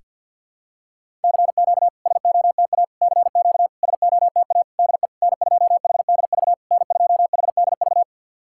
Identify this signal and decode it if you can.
Morse code, heard as a keyed tone, about 50 wpm, 690 Hz. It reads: CQ SOTA CQ SOTA DE N2HDV N2HDV
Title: SOTA QSO Practice 50wpm